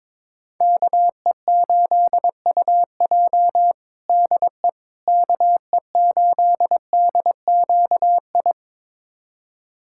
Morse code, heard as a keyed tone, about 22 wpm, 700 Hz. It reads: KE8UJ DE KE8DQI
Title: USA Callsign DE Callsign 22wpm